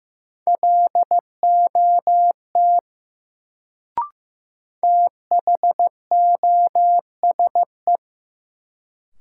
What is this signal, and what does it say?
Morse code, heard as a keyed tone, about 15 wpm, 700 Hz. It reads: LOT THOSE
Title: Sets of 2 Words - New Word Drill - 15wpm